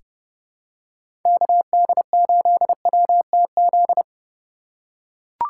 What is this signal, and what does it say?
Morse code, heard as a keyed tone, about 30 wpm, 700 Hz. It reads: KD8WTZ
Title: USA Call Signs - 2x Character Spacing - 30wpm